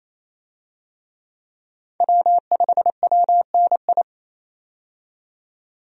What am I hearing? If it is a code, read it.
Morse code, heard as a keyed tone, about 27 wpm, 700 Hz. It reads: W5WNI
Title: USA Call Signs - 5 Characters - 28wpm